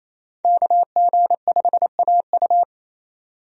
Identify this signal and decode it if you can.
Morse code, heard as a keyed tone, about 28 wpm, 700 Hz. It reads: KG5AU
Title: USA Call Signs - 5 Characters - 28wpm